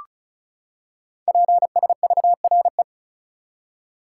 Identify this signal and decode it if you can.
Morse code, heard as a keyed tone, about 36 wpm, 700 Hz. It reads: PSVRE
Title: Code Groups - 5 Letter:Numbers - 35wpm